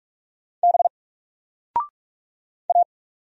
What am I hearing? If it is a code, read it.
Morse code, heard as a keyed tone, about 44 wpm, 700 Hz. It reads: B A
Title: Single Letter-Number Alt 45wpm